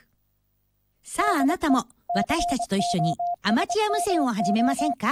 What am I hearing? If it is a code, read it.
Morse code, heard as a keyed tone, about 24 wpm, 700 Hz, with speech. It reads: DE COSMO